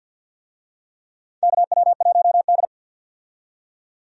Morse code, heard as a keyed tone, about 52 wpm, 700 Hz. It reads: KW1D
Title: USA Call Signs - 4 Characters - 50wpm